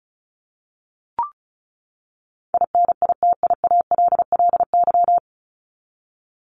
Morse code, heard as a keyed tone, about 34 wpm, 700 Hz. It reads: INITIALLY
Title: Top 1251-1500 Words 35wpm